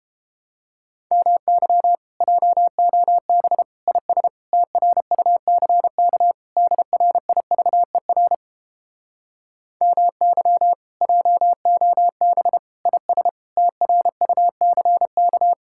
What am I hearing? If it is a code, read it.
Morse code, heard as a keyed tone, about 29 wpm, 700 Hz. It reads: MY JOB IS TRUCK DRIVER MY JOB IS TRUCK